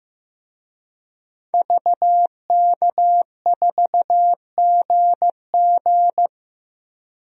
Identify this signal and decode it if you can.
Morse code, heard as a keyed tone, about 15 wpm, 700 Hz. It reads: VK4GG
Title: DXCC-100 Call Signs - 5 Characters - 15wpm